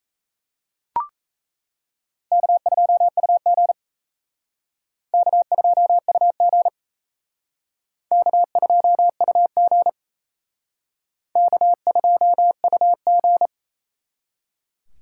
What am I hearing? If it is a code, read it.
Morse code, heard as a keyed tone, about 35 wpm, 700 Hz. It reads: K2UG K2UG K2UG K2UG
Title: Speed Racing - USA 1x2 Call Signs 28wpm